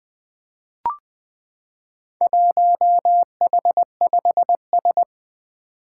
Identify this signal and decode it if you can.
Morse code, heard as a keyed tone, about 20 wpm, 700 Hz. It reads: 1H5S
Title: Four Challenging Letter-Number Combinations 20wpm